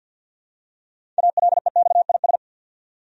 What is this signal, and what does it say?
Morse code, heard as a keyed tone, about 52 wpm, 700 Hz. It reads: ALEXIS
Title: USA Common Names - 50wpm